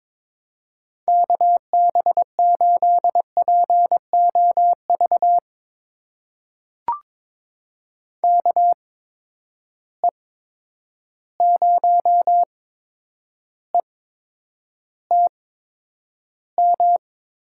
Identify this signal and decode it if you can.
Morse code, heard as a keyed tone, about 22 wpm, spaced as for 9 wpm, 700 Hz. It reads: KB8POV KE0ETM
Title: USA Call Signs - 6 Characters - 8x Character Spacing - 22wpm